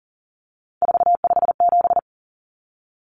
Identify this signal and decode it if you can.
Morse code, heard as a keyed tone, about 40 wpm, 700 Hz. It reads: 457
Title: Numbers - 3-Digit 40wpm